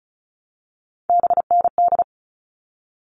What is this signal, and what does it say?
Morse code, heard as a keyed tone, about 34 wpm, 700 Hz. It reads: BND
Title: All Cut Numbers - 3-Digit 35wpm